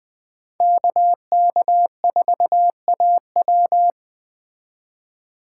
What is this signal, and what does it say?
Morse code, heard as a keyed tone, about 20 wpm, 700 Hz. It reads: KK4AW
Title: Speed Racing - USA Call Signs 17wpm